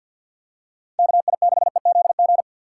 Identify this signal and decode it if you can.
Morse code, heard as a keyed tone, about 52 wpm, 700 Hz. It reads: KI6EZG
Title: USA Call Signs - 6 Characters - 4x Character Spacing - 50wpm